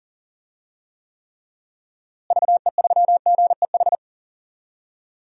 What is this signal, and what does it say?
Morse code, heard as a keyed tone, about 39 wpm, 700 Hz. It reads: VE3GEH
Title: DXCC-100 Call Signs - 6 Characters - Repeated 3x - 40wpm